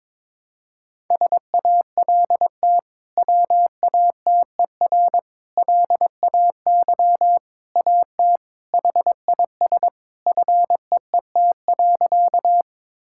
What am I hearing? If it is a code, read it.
Morse code, heard as a keyed tone, about 22 wpm, 700 Hz. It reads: SALT WATER LAY AT HIS FEET.
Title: Sentences from Top 1000 Words - Review Part 5 - 22wpm